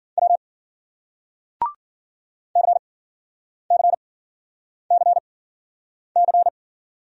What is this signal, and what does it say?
Morse code, heard as a keyed tone, about 56 wpm, spaced as for 9 wpm, 700 Hz. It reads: W CCCC